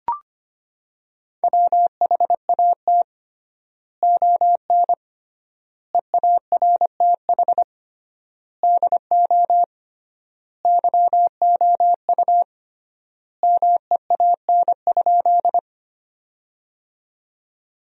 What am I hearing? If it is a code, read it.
Morse code, heard as a keyed tone, about 25 wpm, 700 Hz. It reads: WHAT ON EARTH DO YOU MEAN?
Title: 3x Word-Spacing - Sentences from Top 200 Words - 25wpm